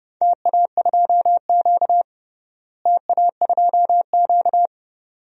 Morse code, heard as a keyed tone, about 30 wpm, 700 Hz. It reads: TA2Q TA2Q
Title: DXCC-100 Call Signs - 4 Characters - Repeated 2x - 30wpm